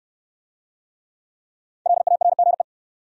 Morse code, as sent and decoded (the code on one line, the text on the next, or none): .-.. .- .-. --. .
LARGE